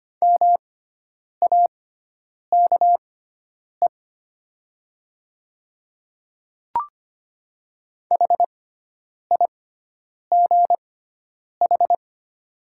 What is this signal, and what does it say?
Morse code, heard as a keyed tone, about 25 wpm, spaced as for 9 wpm, 690 Hz. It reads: MAKE HIGH
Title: 6x Character Spacing - Top 200 Words - 25wpm